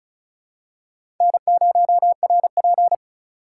Morse code, heard as a keyed tone, about 36 wpm, 700 Hz. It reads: N0RP
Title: USA Call Signs - 4 Characters - 3x Character Spacing - 35wpm